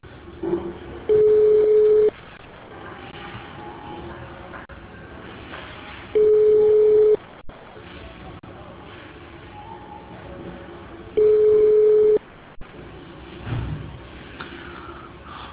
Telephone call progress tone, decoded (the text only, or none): ringback tone